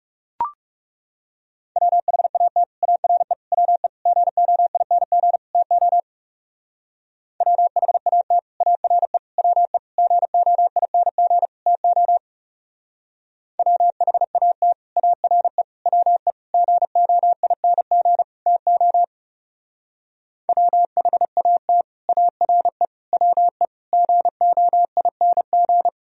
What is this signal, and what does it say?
Morse code, heard as a keyed tone, about 37 wpm, 700 Hz. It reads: WHAT ARE WE GOING TO WHAT ARE WE GOING TO WHAT ARE WE GOING TO WHAT ARE WE GOING
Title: Speed-Racing - Sets of 5 Words 30wpm